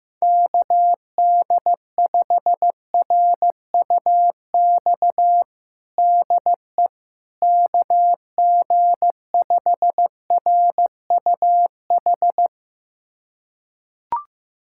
Morse code, heard as a keyed tone, about 15 wpm, 700 Hz. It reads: KD5RUX DE KG5RUH
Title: USA Callsign DE Callsign - 6 Characters - 15wpm